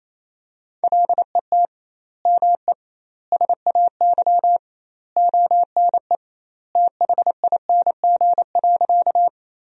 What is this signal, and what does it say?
Morse code, heard as a keyed tone, about 28 wpm, 700 Hz. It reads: LET ME SAY ONE THING.
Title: 2x Word-Spacing - Sentences from Top 300 Words - 28wpm